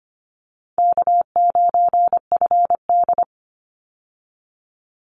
Morse code, heard as a keyed tone, about 25 wpm, 700 Hz. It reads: K9FD
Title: Call Signs Repeated 2x - 25wpm